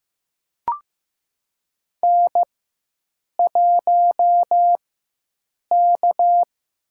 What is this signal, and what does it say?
Morse code, heard as a keyed tone, about 15 wpm, spaced as for 7 wpm, 700 Hz. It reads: N1K
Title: USA Call Signs - 4x Character Spacing - 15wpm